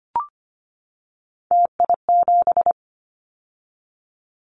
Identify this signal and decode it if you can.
Morse code, heard as a keyed tone, about 25 wpm, 700 Hz. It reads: TI7
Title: Callsign Prefixes for Top 100 DXCC - 25wpm